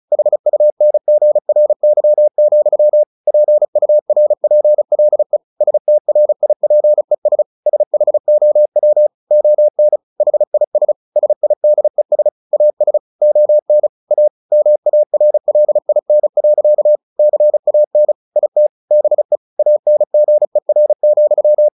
Morse code, heard as a keyed tone, about 34 wpm, 600 Hz. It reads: HUNGRY, PURPLE STRIPES SHOW ON HIS SIDES AS ON A MARLIN. CAN IT BE ANGER,
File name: the old man and the sea 2 35wpm